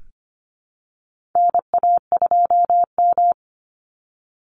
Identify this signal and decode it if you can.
Morse code, heard as a keyed tone, about 25 wpm, 700 Hz. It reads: NA2M
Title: Call Signs - 25wpm